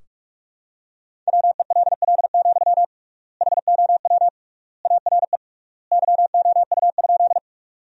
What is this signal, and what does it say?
Morse code, heard as a keyed tone, about 41 wpm, 700 Hz. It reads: WELL, HOW ARE YOU?
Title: 3x Word-Spacing - Sentences from Top 200 Words - 45wpm